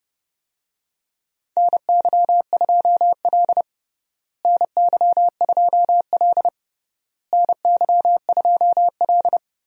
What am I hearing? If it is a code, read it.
Morse code, heard as a keyed tone, about 30 wpm, 700 Hz. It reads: NY2L NY2L NY2L
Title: USA Call Signs - 4 Characters - Repeated 3x - 30wpm